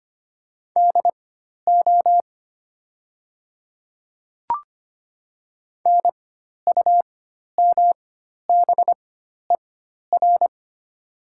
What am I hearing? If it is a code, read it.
Morse code, heard as a keyed tone, about 25 wpm, spaced as for 12 wpm, 700 Hz. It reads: DO NUMBER
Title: 4x Character Spacing - Top 100 Words - 25wpm